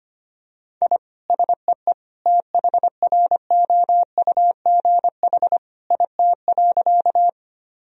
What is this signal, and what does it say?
Morse code, heard as a keyed tone, about 25 wpm, 700 Hz. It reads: I SEE THROUGH IT.
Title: Sentences from Top 1000 Words - Review Part 1 - 25wpm